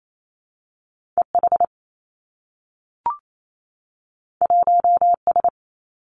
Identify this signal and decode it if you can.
Morse code, heard as a keyed tone, about 28 wpm, 700 Hz. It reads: E H 1 S
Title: Two Challenging Letter-Number Combinations 28wpm